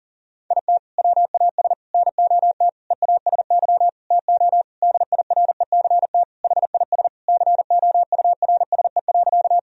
Morse code, heard as a keyed tone, about 40 wpm, 700 Hz. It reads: IT WAS NOT EASY TO DIRECT HIS COURSE.
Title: Sentences from Top 1000 Words - Review Part 2 - 40wpm